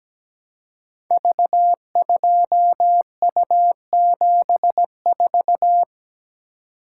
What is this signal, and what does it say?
Morse code, heard as a keyed tone, about 17 wpm, 700 Hz. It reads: V2U74
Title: Code Groups - 5 Letter:Numbers - 17wpm